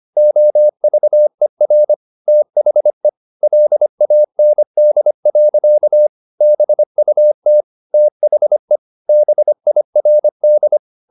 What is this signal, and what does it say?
Morse code, heard as a keyed tone, about 25 wpm, 600 Hz. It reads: OVER THE LAND. BUT THE BIRD
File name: the old man and the sea part 1 25wpm